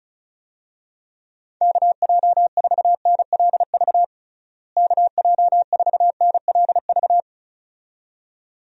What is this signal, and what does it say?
Morse code, heard as a keyed tone, about 34 wpm, 700 Hz. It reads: KJ4NLV KJ4NLV
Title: USA Call Signs - 6 Characters - Repeated 2x - 35wpm